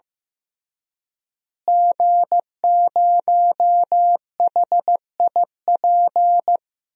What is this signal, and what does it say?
Morse code, heard as a keyed tone, about 15 wpm, 700 Hz. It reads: G0HIP
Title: DXCC-100 Call Signs - 5 Characters - Repeated 3x - 15wpm